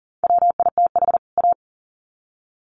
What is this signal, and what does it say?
Morse code, heard as a keyed tone, about 40 wpm, 700 Hz. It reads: WITH A
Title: Sets of 2 Words 40wpm